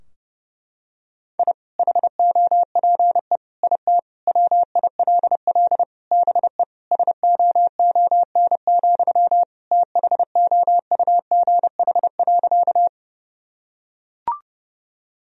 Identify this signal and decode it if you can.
Morse code, heard as a keyed tone, about 30 wpm, 700 Hz. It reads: I HOPE IT WILL BE SOON, THOUGH.
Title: Sentences from Top 1000 Words - Review Part 2 - 30wpm